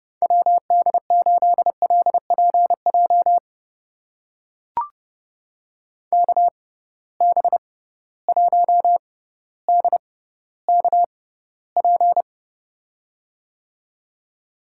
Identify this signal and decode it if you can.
Morse code, heard as a keyed tone, about 30 wpm, spaced as for 15 wpm, 700 Hz. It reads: WD8LPJ KB1DKP
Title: USA Call Signs - 6 Characters - 6x Character Spacing - 30wpm